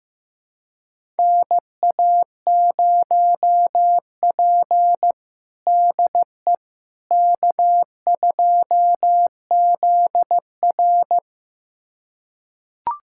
Morse code, heard as a keyed tone, about 15 wpm, 700 Hz. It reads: NA0P DE K2ZR
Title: USA Callsign DE Callsign - 4 Characters - 15wpm